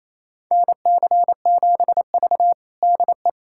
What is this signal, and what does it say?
Morse code, heard as a keyed tone, about 28 wpm, 700 Hz. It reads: NC7V DE
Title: USA Callsign DE Callsign - 4 Characters - 28wpm